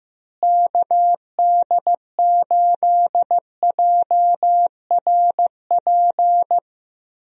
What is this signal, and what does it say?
Morse code, heard as a keyed tone, about 15 wpm, 700 Hz. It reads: KD8JRP
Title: USA Call Signs - 6 Characters - Repeated 3x - 15wpm